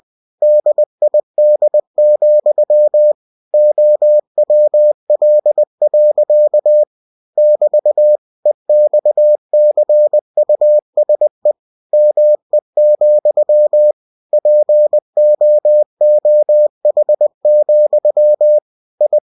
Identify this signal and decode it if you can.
Morse code, heard as a keyed tone, about 20 wpm, 600 Hz. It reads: DID, OWL. <BT> EXCUSE ME, POOH, I
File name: cw-books / winnie-the-pooh / winnie-20wpm-page-0030